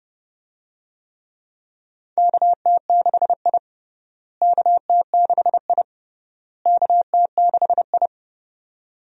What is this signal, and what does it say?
Morse code, heard as a keyed tone, about 30 wpm, 700 Hz. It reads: KT6I KT6I KT6I
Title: USA Call Signs - 4 Characters - Repeated 3x - 30wpm